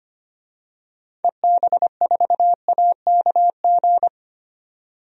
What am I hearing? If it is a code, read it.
Morse code, heard as a keyed tone, about 25 wpm, 700 Hz. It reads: EB4AKG
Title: DXCC-100 Call Signs - 6 Characters - 25wpm